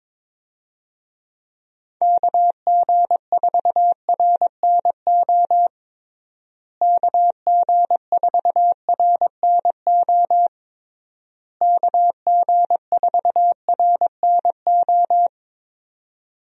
Morse code, heard as a keyed tone, about 22 wpm, 700 Hz. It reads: KG4RNO KG4RNO KG4RNO
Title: USA Call Signs - 6 Characters - Repeated 3x - 22wpm